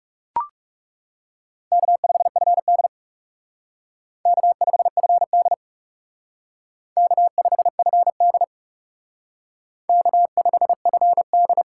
Morse code, heard as a keyed tone, about 38 wpm, 700 Hz. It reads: K5FD K5FD K5FD K5FD